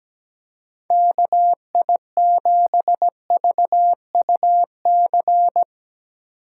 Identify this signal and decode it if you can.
Morse code, heard as a keyed tone, about 18 wpm, 700 Hz. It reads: KI7VUC
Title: Speed Racing - USA 2x3 Call Signs 17wpm